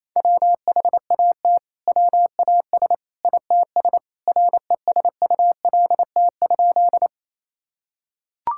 Morse code, heard as a keyed tone, about 28 wpm, 700 Hz. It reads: WHAT WAS ITS RESULT?
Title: Sentences from Top 1000 Words - Review Part 5 - 28wpm